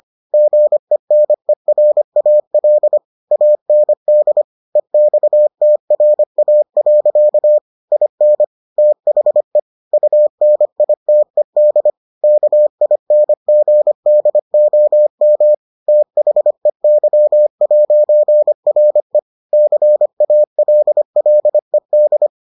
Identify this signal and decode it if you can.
Morse code, heard as a keyed tone, about 25 wpm, 600 Hz. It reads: GENERAL AND EXTRA. IN THE UNITED KINGDOM THEY'RE CALLED